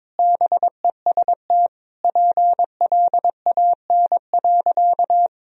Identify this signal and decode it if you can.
Morse code, heard as a keyed tone, about 21 wpm, 700 Hz. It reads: BEST PLAN.
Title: Sentences from Top 1000 Words - Review Part 2 - 22wpm